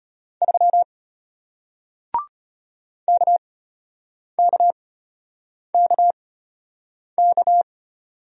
Morse code, heard as a keyed tone, about 34 wpm, spaced as for 8 wpm, 700 Hz. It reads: 3 KKKK